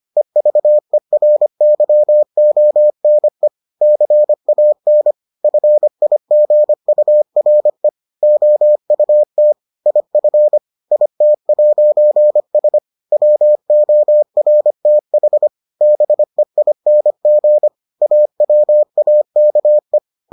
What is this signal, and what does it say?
Morse code, heard as a keyed tone, about 25 wpm, 600 Hz. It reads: EVERYONE CAN FIGURE OUT IF IT'S WORTH BEING AWAKE